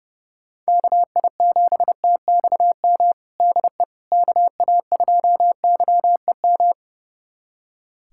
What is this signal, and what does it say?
Morse code, heard as a keyed tone, about 31 wpm, 700 Hz. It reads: KI7TXM DE KA2YEM
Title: USA Callsign DE Callsign - 6 Characters - 30wpm